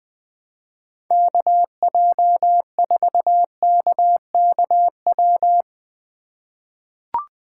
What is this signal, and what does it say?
Morse code, heard as a keyed tone, about 20 wpm, 700 Hz. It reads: KJ4KKW
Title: USA Call Signs - 2x Character Spacing - 20wpm